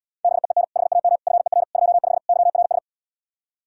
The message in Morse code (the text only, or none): BEEN HAM FER 25 YRS